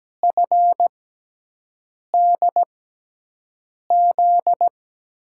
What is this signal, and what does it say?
Morse code, heard as a keyed tone, about 17 wpm, spaced as for 6 wpm, 700 Hz. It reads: FDZ
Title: USA Call Signs - 6 Characters - 6x Character Spacing - 17wpm